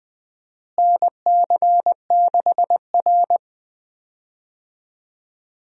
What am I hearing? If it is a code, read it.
Morse code, heard as a keyed tone, about 20 wpm, 690 Hz. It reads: NC6R
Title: USA Call Signs - 4 Characters - Repeated 3x - 20wpm